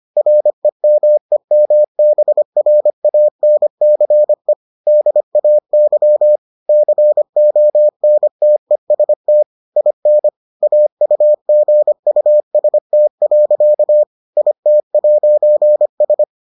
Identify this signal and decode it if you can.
Morse code, heard as a keyed tone, about 25 wpm, 600 Hz. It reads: REMEMBRANCE DAY CONTEST IN AUGUST. IT'S